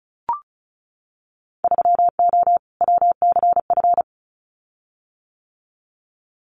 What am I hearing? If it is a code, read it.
Morse code, heard as a keyed tone, about 36 wpm, 700 Hz. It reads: 3O WCF
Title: Winter Field Day Exchanges Encore 35wpm